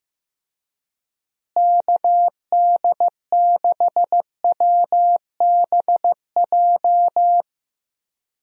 Morse code, heard as a keyed tone, about 15 wpm, 700 Hz. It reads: KD6WBJ
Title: USA Call Signs - 6 Characters - Repeated 3x - 15wpm